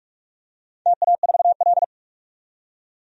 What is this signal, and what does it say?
Morse code, heard as a keyed tone, about 46 wpm, 700 Hz. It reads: TA4L
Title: DXCC-100 Call Signs - 4 Characters - 45wpm